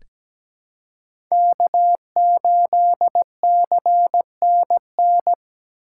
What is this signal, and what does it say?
Morse code, heard as a keyed tone, about 17 wpm, 700 Hz. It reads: K8CNN
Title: USA Call Signs - 5 Characters - 3x Character Spacing - 17wpm